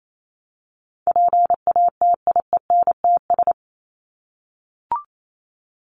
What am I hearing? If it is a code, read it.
Morse code, heard as a keyed tone, about 28 wpm, 700 Hz. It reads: PATIENTS
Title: ES:S Suffix - Top 100 Words with Base Words 28wpm